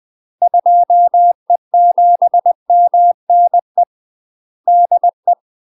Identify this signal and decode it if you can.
Morse code, heard as a keyed tone, about 20 wpm, 700 Hz. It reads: 2E7MNE DE
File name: UK MASTER QSOs TRAINING/20WPM